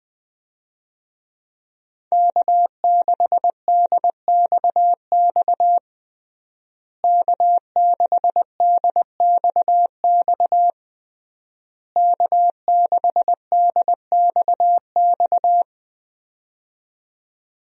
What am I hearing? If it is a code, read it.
Morse code, heard as a keyed tone, about 20 wpm, 700 Hz. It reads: K6DXX K6DXX K6DXX
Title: USA Call Signs - 5 Characters - Repeated 3x - 20wpm